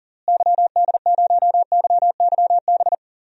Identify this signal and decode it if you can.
Morse code, heard as a keyed tone, about 40 wpm, 700 Hz. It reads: YD0YYB
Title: DXCC-100 Call Signs - 6 Characters - Repeated 3x - 40wpm